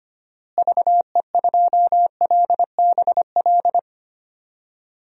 Morse code, heard as a keyed tone, about 25 wpm, 700 Hz. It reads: VE2LBL
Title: DXCC-100 Call Signs - 6 Characters - Repeated 3x - 25wpm